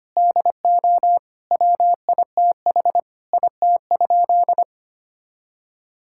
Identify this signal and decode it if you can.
Morse code, heard as a keyed tone, about 25 wpm, 700 Hz. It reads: DO WITH IT?
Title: Sentences from Top 1000 Words - Review Part 5 - 25wpm